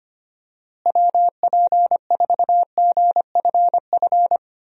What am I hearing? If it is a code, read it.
Morse code, heard as a keyed tone, about 25 wpm, 700 Hz. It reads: WP4GFF
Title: USA Call Signs - 6 Characters - Repeated 3x - 25wpm